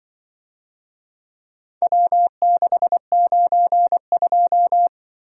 Morse code, heard as a keyed tone, about 23 wpm, 700 Hz. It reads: W692
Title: Speed-Racing - Four Letter-Number Combinations - 20wpm